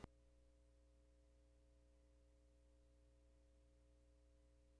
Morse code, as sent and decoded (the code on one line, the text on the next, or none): none